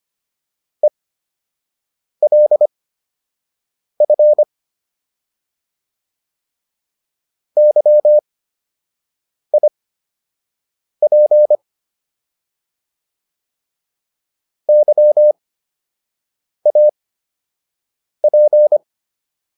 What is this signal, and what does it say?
Morse code, heard as a keyed tone, about 25 wpm, spaced as for 6 wpm, 600 Hz. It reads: ELF YIP YAP